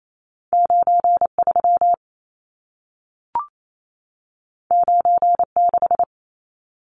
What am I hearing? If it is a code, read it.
Morse code, heard as a keyed tone, about 28 wpm, 700 Hz. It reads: 93 96
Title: Numbers - 2-Digit 28wpm